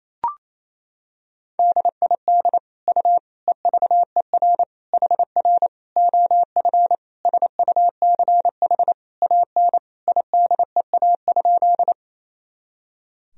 Morse code, heard as a keyed tone, about 28 wpm, 700 Hz. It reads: DID U EVER HR OF SUCH AN IDEA?
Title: Sentences from Top 400 Words and Abbreviations 28wpm